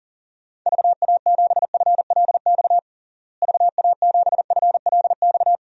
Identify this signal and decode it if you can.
Morse code, heard as a keyed tone, about 37 wpm, 700 Hz. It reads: VA7FLX VA7FLX
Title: DXCC-100 Call Signs - 6 Characters - Repeated 2x - 40wpm